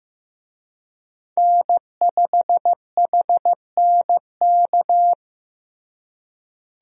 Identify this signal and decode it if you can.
Morse code, heard as a keyed tone, about 15 wpm, 700 Hz. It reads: N5HNK
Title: USA Call Signs - 5 Characters - Repeated 2x - 15wpm